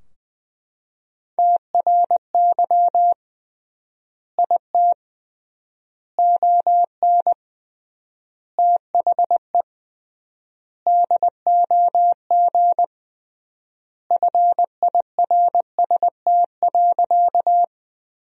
Morse code, heard as a keyed tone, about 20 wpm, 700 Hz. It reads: TRY IT ON THE DOG FIRST.
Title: 3x Word-Spacing - Sentences from Top 400 Words - 20wpm